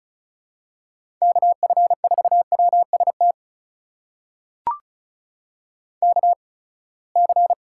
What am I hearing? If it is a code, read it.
Morse code, heard as a keyed tone, about 36 wpm, spaced as for 24 wpm, 700 Hz. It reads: KF4WST KC
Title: USA Call Signs - 6 Characters - 8x Character Spacing - 35wpm